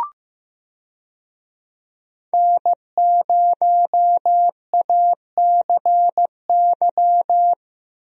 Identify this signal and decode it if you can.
Morse code, heard as a keyed tone, about 15 wpm, 700 Hz. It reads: N0ACY
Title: USA Call Signs - 5 Characters - Repeated 2x - 15wpm